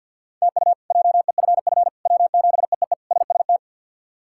Morse code, heard as a keyed tone, about 51 wpm, 700 Hz. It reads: TU JEFF W7EEE SST
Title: K1USN SST QSO Practice - 50wpm